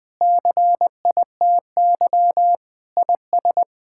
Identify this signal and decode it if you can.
Morse code, heard as a keyed tone, about 20 wpm, 700 Hz. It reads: CITY IS
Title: Sentences with 7-Word Phrases - 20wpm